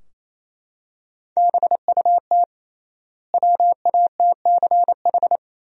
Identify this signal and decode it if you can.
Morse code, heard as a keyed tone, about 28 wpm, 700 Hz. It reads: BUT WATCH
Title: 3x Word-Spacing - Sentences from Top 300 Words - 28wpm